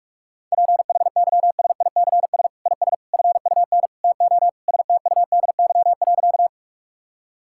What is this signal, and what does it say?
Morse code, heard as a keyed tone, about 46 wpm, 700 Hz. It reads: PHYSICS IS FUN TO STUDY.
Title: Sentences from Top 1000 Words - Review Part 5 - 45wpm